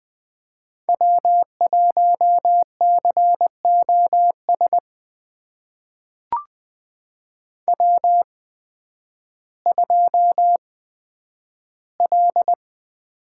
Morse code, heard as a keyed tone, about 20 wpm, spaced as for 11 wpm, 700 Hz. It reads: W1COS W2L
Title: USA Call Signs - 5 Characters - 8x Character Spacing - 20wpm